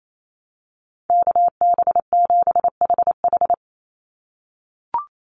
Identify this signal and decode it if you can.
Morse code, heard as a keyed tone, about 28 wpm, 700 Hz. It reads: KB7HH
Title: NA SOTA Chaser Callsigns 28wpm